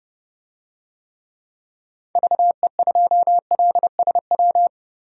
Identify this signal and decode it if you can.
Morse code, heard as a keyed tone, about 30 wpm, 700 Hz. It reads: VE2LSW
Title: DXCC-100 Call Signs - 6 Characters - Repeated 2x - 30wpm